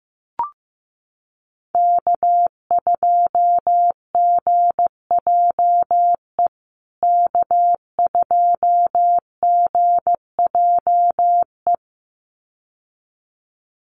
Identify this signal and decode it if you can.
Morse code, heard as a keyed tone, about 15 wpm, 700 Hz. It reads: K2GJE K2GJE
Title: USA Call Signs Repeated 2x 15wpm